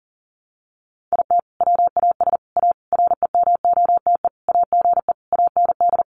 Morse code, heard as a keyed tone, about 40 wpm, 700 Hz. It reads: IT WAS A REMOTE AGE AND
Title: Sentences from Top 1300 Words 40wpm